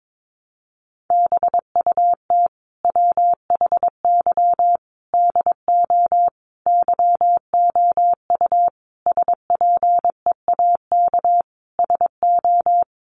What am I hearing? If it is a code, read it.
Morse code, heard as a keyed tone, about 22 wpm, 700 Hz. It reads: BUT WHY DO YOU SPEAK SO
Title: Sentences from Top 1200 Words 22wpm